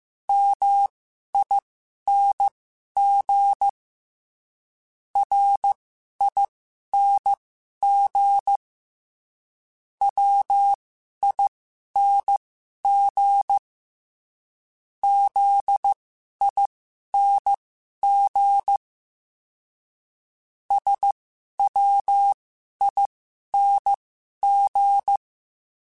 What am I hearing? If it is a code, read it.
Morse code, heard as a keyed tone, about 15 wpm, spaced as for 11 wpm, 800 Hz. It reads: MING RING WING ZING SWING
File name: Special INGs-mp3s / Special INGs-mp3s / Sngl Ing @10